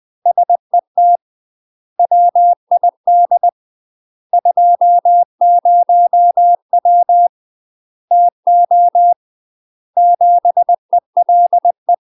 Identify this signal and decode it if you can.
Morse code, heard as a keyed tone, about 20 wpm, 700 Hz. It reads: SET WID 20W TO 7 ELE
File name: UK MASTER QSOs TRAINING / 20WPM / UK MASTER QSOs 0028